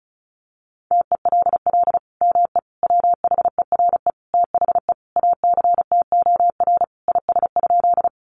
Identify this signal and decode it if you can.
Morse code, heard as a keyed tone, about 34 wpm, 700 Hz. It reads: TELL ME WHERE THE ACTOR IS?
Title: Sentences from Top 1500 Words 35wpm